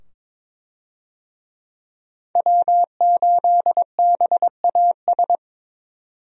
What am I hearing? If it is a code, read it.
Morse code, heard as a keyed tone, about 22 wpm, 700 Hz. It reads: W8BAS
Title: USA Call Signs - 5 Characters - 22wpm